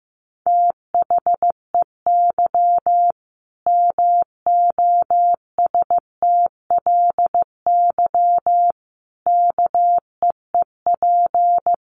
Morse code, heard as a keyed tone, about 15 wpm, 700 Hz. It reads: THEY MOSTLY KEEP
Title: Sentences from Top 1300 Words 15wpm